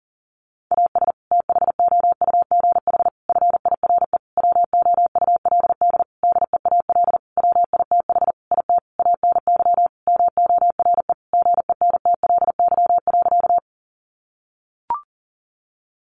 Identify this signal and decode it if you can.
Morse code, heard as a keyed tone, about 40 wpm, 700 Hz. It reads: AS THOUGH FIRE WOULD DEAL WITH IT ANY MORE GENTLY.
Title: Sentences from Top 1500 Words 40wpm